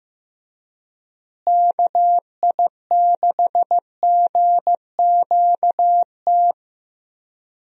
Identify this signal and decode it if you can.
Morse code, heard as a keyed tone, about 15 wpm, 700 Hz. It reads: KI6GQT
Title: USA Call Signs - 6 Characters - Repeated 2x - 15wpm